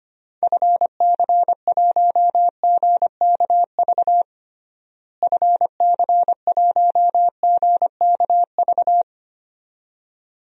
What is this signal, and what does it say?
Morse code, heard as a keyed tone, about 25 wpm, 700 Hz. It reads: FC1GKV FC1GKV
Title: DXCC-100 Call Signs - 6 Characters - Repeated 2x - 25wpm